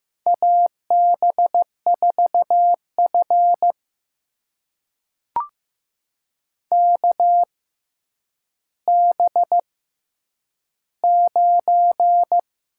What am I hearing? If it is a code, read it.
Morse code, heard as a keyed tone, about 15 wpm, spaced as for 9 wpm, 700 Hz. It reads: AB4F KB9
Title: USA Call Signs - 6x Character Spacing - 15wpm